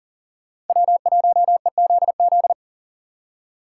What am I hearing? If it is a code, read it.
Morse code, heard as a keyed tone, about 52 wpm, 700 Hz. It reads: W1EZZ